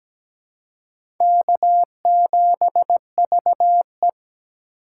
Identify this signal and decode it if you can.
Morse code, heard as a keyed tone, about 17 wpm, 700 Hz. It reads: K7VE
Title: USA Call Signs - 4 Characters - 2x Character Spacing - 17wpm